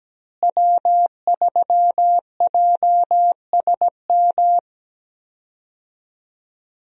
Morse code, heard as a keyed tone, about 17 wpm, 700 Hz. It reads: W3JSM
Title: USA Call Signs - 5 Characters - Repeated 3x - 17wpm